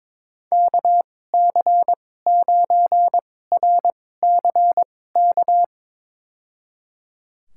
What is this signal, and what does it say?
Morse code, heard as a keyed tone, about 21 wpm, spaced as for 16 wpm, 700 Hz. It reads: KC9RCK
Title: USA Call Signs - 6 Characters - 2x Character Spacing - 22wpm